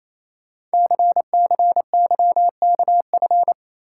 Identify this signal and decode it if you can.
Morse code, heard as a keyed tone, about 29 wpm, 700 Hz. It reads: CCYKF
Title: Code Groups - 5 Letters - 28wpm